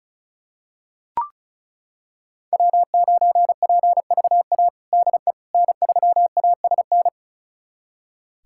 Morse code, heard as a keyed tone, about 34 wpm, 700 Hz. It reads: W9PVA DE N3ASN
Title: USA Callsign DE Callsign - 5 Characters - 35wpm